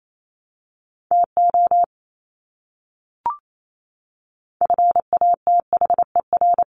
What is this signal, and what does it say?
Morse code, heard as a keyed tone, about 28 wpm, 700 Hz. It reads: TO FATHER
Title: Top 300 Words Repeated 28wpm